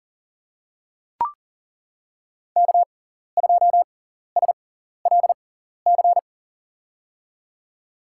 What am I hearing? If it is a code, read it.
Morse code, heard as a keyed tone, about 40 wpm, spaced as for 14 wpm, 700 Hz. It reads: K2SLC
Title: USA Call Signs - 5 Characters - 6x Character Spacing - 40wpm